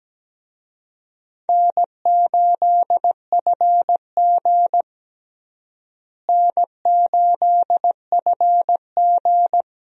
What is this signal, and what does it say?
Morse code, heard as a keyed tone, about 17 wpm, 700 Hz. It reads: N8FG N8FG
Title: USA Call Signs - 4 Characters - Repeated 2x - 17wpm